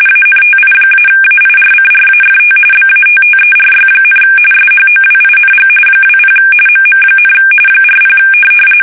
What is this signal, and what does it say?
RTTY, about 45.45 baud. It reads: ANTI-AIRCRAFT BATTERY, SHOWING 20 MM, 40
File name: fsk850 / USS-Massachusetts